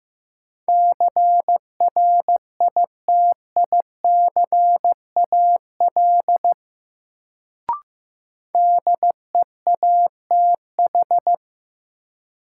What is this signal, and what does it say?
Morse code, heard as a keyed tone, about 15 wpm, 700 Hz. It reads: CRITICAL DEATH
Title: Sentences from Top 1100 Words - New Word Drill - 15wpm